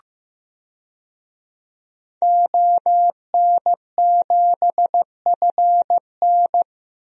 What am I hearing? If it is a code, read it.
Morse code, heard as a keyed tone, about 15 wpm, 700 Hz. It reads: ON7FN
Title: DXCC-100 Call Signs - 5 Characters - Repeated 2x - 15wpm